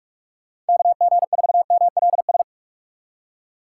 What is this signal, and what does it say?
Morse code, heard as a keyed tone, about 56 wpm, 700 Hz. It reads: KG4MLS
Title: Speed Racing - USA Call Signs 45wpm